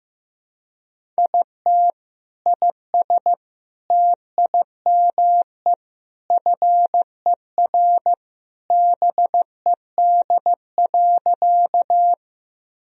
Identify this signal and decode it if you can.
Morse code, heard as a keyed tone, about 16 wpm, 700 Hz. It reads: IT IS TIME FER BED.
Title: Sentences from Top 500 Words and Abbreviations 15wpm